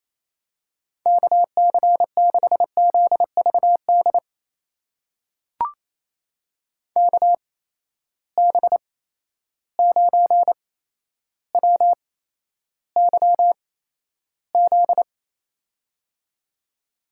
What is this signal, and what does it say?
Morse code, heard as a keyed tone, about 28 wpm, spaced as for 12 wpm, 700 Hz. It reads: KC6ZVD KB9WYZ
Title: USA Call Signs - 6 Characters - 8x Character Spacing - 28wpm